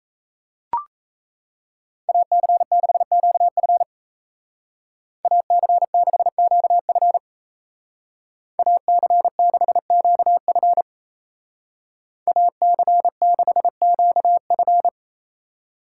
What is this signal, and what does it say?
Morse code, heard as a keyed tone, about 37 wpm, 700 Hz. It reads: AC6QF AC6QF AC6QF AC6QF